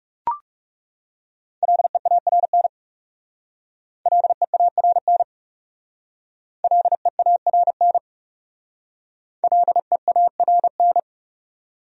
Morse code, heard as a keyed tone, about 36 wpm, 700 Hz. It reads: LEARN LEARN LEARN LEARN
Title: Speed-Racing Top 300 Words 30wpm